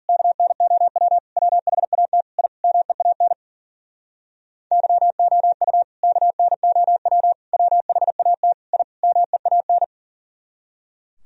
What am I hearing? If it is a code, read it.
Morse code, heard as a keyed tone, about 52 wpm, 700 Hz. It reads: KNOW WHAT I MEAN YOU KNOW WHAT I MEAN